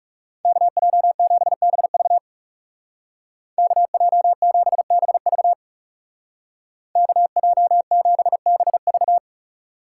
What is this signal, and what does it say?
Morse code, heard as a keyed tone, about 35 wpm, 700 Hz. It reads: KJ7BV KJ7BV KJ7BV
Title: Speed Racing - USA 2x2 Call Signs 30wpm